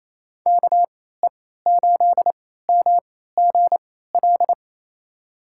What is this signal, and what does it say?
Morse code, heard as a keyed tone, about 28 wpm, spaced as for 16 wpm, 700 Hz. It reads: KE8MGL
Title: USA Call Signs - 3x Character Spacing - 28wpm